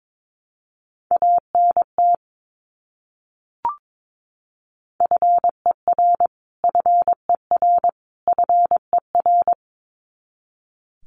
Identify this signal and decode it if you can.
Morse code, heard as a keyed tone, about 22 wpm, 700 Hz. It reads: ANT FER FER FER
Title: Abbreviations Repeated 3x - 22wpm